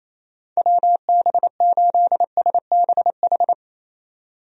Morse code, heard as a keyed tone, about 28 wpm, 700 Hz. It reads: WB8SBH
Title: USA Call Signs - 6 Characters - Repeated 3x - 28wpm